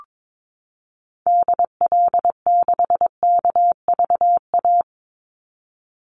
Morse code, heard as a keyed tone, about 22 wpm, 700 Hz. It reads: DL6KVA
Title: Call Signs Repeated 2x - 22wpm